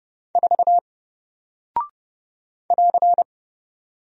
Morse code, heard as a keyed tone, about 30 wpm, 700 Hz. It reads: 4 <AR>
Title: Single Character (Letter:Number:Punctuation:Prosign) Alt 30wpm